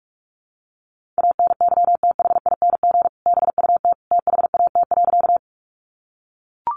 Morse code, heard as a keyed tone, about 44 wpm, 700 Hz. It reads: ANYTHING BUT THAT.
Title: Sentences from Top 1000 Words 45wpm